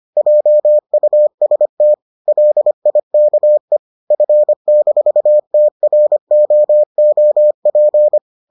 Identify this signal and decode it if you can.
Morse code, heard as a keyed tone, about 25 wpm, 600 Hz. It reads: JUST LIKE F-TROOP